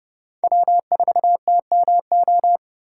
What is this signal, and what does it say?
Morse code, heard as a keyed tone, about 30 wpm, 700 Hz. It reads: W4TMO
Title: K1USN SST QSO Practice - 30wpm